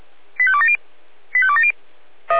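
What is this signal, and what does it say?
Five-tone selective call (ZVEI), heard as quick runs of five tones, tones of about 70 ms. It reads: 85270 85270
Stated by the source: Five-tone ZVEI call received on an SDR.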